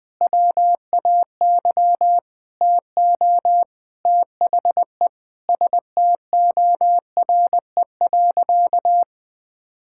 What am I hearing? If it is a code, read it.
Morse code, heard as a keyed tone, about 20 wpm, 700 Hz. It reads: WAY TO THE STORE.
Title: Sentences with 9-Word Phrases - 20wpm